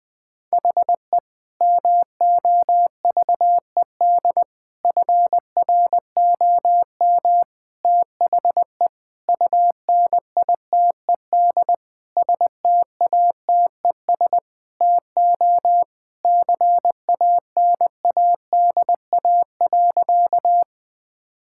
Morse code, heard as a keyed tone, about 20 wpm, 700 Hz. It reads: HE MOVED FROM THE UNITED STATES TO CANADA.
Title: Sentences with 4-Word Phrases - 20wpm